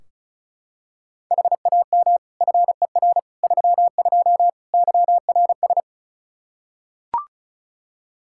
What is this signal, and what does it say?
Morse code, heard as a keyed tone, about 36 wpm, 700 Hz. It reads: HAM FER 32 YRS
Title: K1USN SST QSO Practice 35wpm